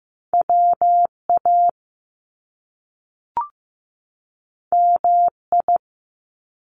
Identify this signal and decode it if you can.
Morse code, heard as a keyed tone, about 15 wpm, 700 Hz. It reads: WA MI
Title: US States 15wpm